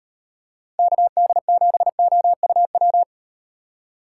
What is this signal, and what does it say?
Morse code, heard as a keyed tone, about 38 wpm, 700 Hz. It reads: KD7OUW